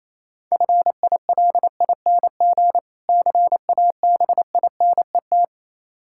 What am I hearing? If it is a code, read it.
Morse code, heard as a keyed tone, about 27 wpm, 700 Hz. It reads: FILING CABINET
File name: Easy Words - Speed-Racing with Hints - 22wpm